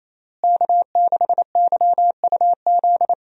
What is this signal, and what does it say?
Morse code, heard as a keyed tone, about 28 wpm, 700 Hz. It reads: K6YUZ
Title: USA Call Signs - 5 Characters - Repeated 2x - 28wpm